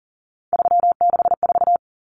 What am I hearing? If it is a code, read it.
Morse code, heard as a keyed tone, about 37 wpm, 700 Hz. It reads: 364
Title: Common Cut Numbers - 3-Digit 40wpm